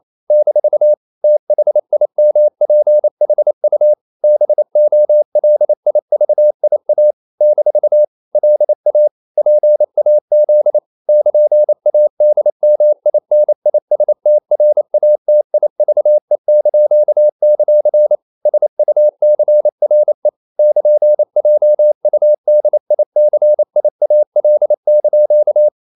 Morse code, heard as a keyed tone, about 28 wpm, 600 Hz. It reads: - THIMPHU BOLIVIA - LA PAZ <KN>ADMINISTRATIVE); SUCRE <KN>JUDICIAL)
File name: CW / Practice / NationalCapitals28